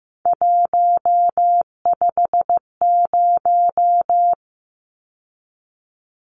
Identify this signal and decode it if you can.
Morse code, heard as a keyed tone, about 15 wpm, 700 Hz. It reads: 150
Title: Numbers - 3-Digit 15wpm